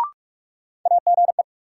Morse code, heard as a keyed tone, about 46 wpm, 700 Hz. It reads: AGE
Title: Sets of 3 Words - New Word Drill - 45wpm